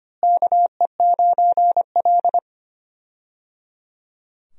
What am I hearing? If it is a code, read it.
Morse code, heard as a keyed tone, about 25 wpm, 700 Hz. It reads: KE9L
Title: USA Call Signs - 4 Characters - Repeated 3x - 25wpm